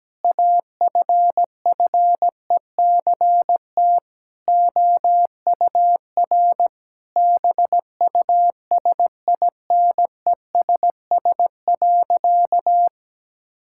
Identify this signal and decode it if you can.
Morse code, heard as a keyed tone, about 17 wpm, 700 Hz. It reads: AFFECT OUR BUSINESS.
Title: Sentences from Top 1000 Words - Review Part 6 - 17wpm